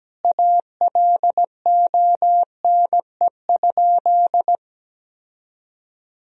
Morse code, heard as a keyed tone, about 17 wpm, 700 Hz. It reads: ALONE?
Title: Sentences from Top 1000 Words - Review Part 4 - 17wpm